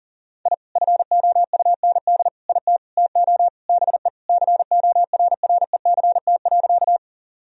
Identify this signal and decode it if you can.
Morse code, heard as a keyed tone, about 37 wpm, 700 Hz. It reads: I FOUND IT TO BE CORRECT.
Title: Sentences from Top 1000 Words - Review Part 2 - 40wpm